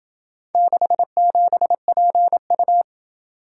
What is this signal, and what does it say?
Morse code, heard as a keyed tone, about 27 wpm, 700 Hz. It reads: 67PU